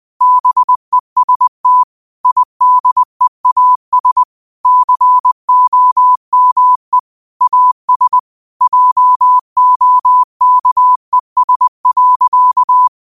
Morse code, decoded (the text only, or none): BEST IDEAS COME AS JOKES.